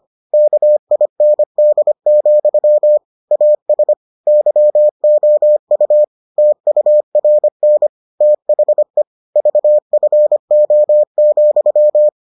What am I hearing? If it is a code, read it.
Morse code, heard as a keyed tone, about 25 wpm, 600 Hz. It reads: KIND, AS YOU TURN THE VFO,